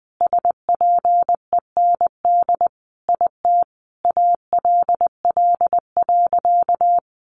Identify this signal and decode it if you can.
Morse code, heard as a keyed tone, about 19 wpm, 700 Hz. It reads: SPEND IT ALL.
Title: Sentences from Top 800 Words 20wpm